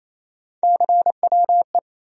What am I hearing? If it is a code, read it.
Morse code, heard as a keyed tone, about 28 wpm, 700 Hz. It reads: CWE
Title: USA Call Sign Postfixes - 3 Letters 28wpm